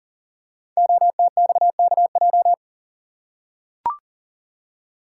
Morse code, heard as a keyed tone, about 38 wpm, 700 Hz. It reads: OTXKJ
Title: Code Groups - 5 Letter:Numbers - 40wpm